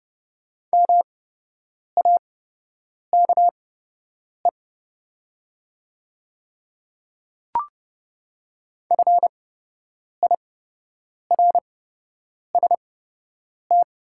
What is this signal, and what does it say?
Morse code, heard as a keyed tone, about 30 wpm, spaced as for 8 wpm, 700 Hz. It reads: MAKE FIRST
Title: 8x Character Spacing - Top 100 Words - 30wpm